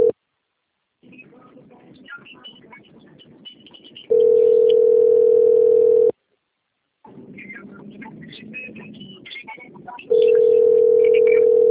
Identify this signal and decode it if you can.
Ringback tone (North America), 440 + 480 Hz.